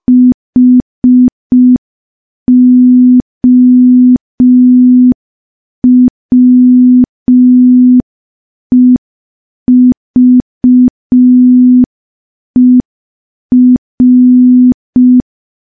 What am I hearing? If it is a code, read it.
Morse code, heard as a keyed tone, about 5 wpm, 250 Hz. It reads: HOWEVER
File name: morse / morse-training-05WPM